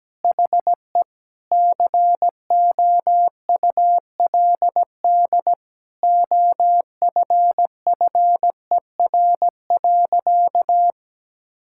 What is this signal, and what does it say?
Morse code, heard as a keyed tone, about 17 wpm, 700 Hz. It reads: HE COULD OFFER.
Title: Sentences from Top 1600 Words 17wpm